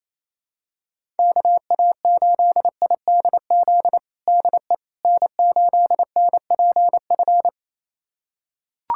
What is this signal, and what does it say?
Morse code, heard as a keyed tone, about 28 wpm, 700 Hz. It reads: KA8IDZ DE N8NPF
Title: USA Callsign DE Callsign 28wpm